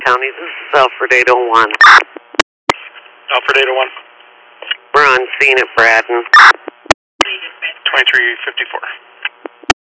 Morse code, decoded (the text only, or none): none